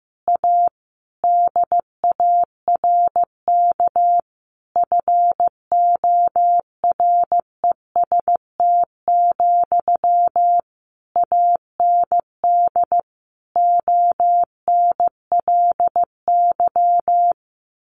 Morse code, heard as a keyed tone, about 15 wpm, 700 Hz. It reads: A DARK FOREST, AND ONLY
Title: A Sound Of Thunder 15wpm